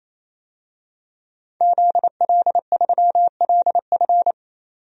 Morse code, heard as a keyed tone, about 27 wpm, 700 Hz. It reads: ZL3LF
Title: DXCC-100 Call Signs - 5 Characters - 28wpm